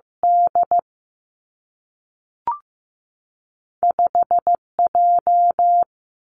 Morse code, heard as a keyed tone, about 15 wpm, 700 Hz. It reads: D 5 J
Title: Two Challenging Letter-Number Combinations 15wpm